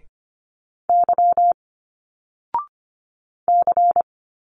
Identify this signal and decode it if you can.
Morse code, heard as a keyed tone, about 25 wpm, 700 Hz. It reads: Y C
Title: Single Letters 25wpm